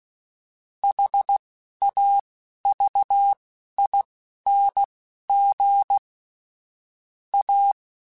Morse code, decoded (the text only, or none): HAVING A